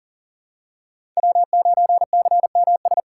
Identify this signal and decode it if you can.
Morse code, heard as a keyed tone, about 40 wpm, 700 Hz. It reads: W9CMS
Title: USA Call Signs - 5 Characters - 4x Character Spacing - 40wpm